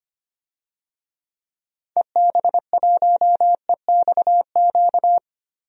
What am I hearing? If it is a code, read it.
Morse code, heard as a keyed tone, about 25 wpm, 700 Hz. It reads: EB1EXQ
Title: DXCC-100 Call Signs - 6 Characters - 25wpm